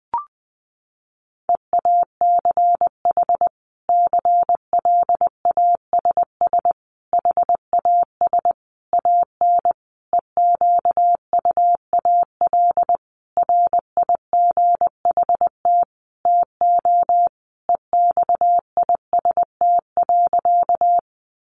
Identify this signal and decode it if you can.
Morse code, heard as a keyed tone, about 20 wpm, 700 Hz. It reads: EACH CLASS HAS AN EQUAL RIGHT TO EXIST.
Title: Sentences from Top 1400 Words 20wpm